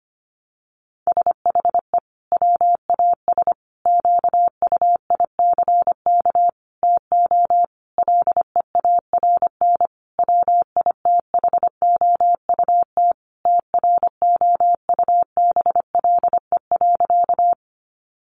Morse code, heard as a keyed tone, about 25 wpm, 700 Hz. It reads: SHE WAS QUICK TO LEARN WITHOUT TROUBLE.
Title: Sentences from Top 1200 Words 25wpm